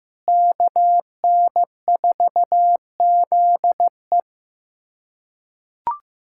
Morse code, heard as a keyed tone, about 15 wpm, 700 Hz. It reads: KN4ZE
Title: USA Callsign DE Callsign - 5 Characters - 15wpm